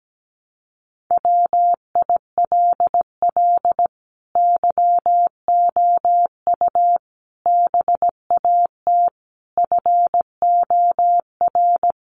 Morse code, heard as a keyed tone, about 17 wpm, 700 Hz. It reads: WILL YOU BAT FOR
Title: Sentences from Top 1500 Words 17wpm